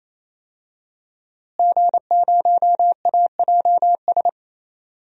Morse code, heard as a keyed tone, about 28 wpm, 700 Hz. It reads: G0AJS
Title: DXCC-100 Call Signs - 5 Characters - Repeated 3x - 28wpm